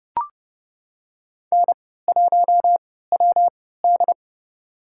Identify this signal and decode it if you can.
Morse code, heard as a keyed tone, about 31 wpm, spaced as for 17 wpm, 700 Hz. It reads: N1WD
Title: USA Call Signs - 4 Characters - 3x Character Spacing - 30wpm